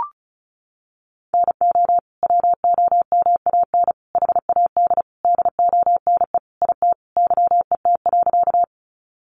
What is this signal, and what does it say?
Morse code, heard as a keyed tone, about 34 wpm, 700 Hz. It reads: NO WOMAN HAD DONE IT YET.
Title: Sentences from Top 1100 Words 35wpm